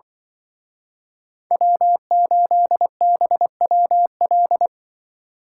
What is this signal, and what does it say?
Morse code, heard as a keyed tone, about 25 wpm, 700 Hz. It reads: W8BWL
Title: Speed Racing - USA 1x3 Call Signs 20wpm